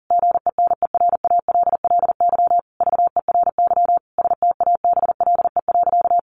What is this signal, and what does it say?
Morse code, heard as a keyed tone, about 40 wpm, 700 Hz. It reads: GENERALLY VERY STABLE.
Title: Sentences from Top 1500 Words 40wpm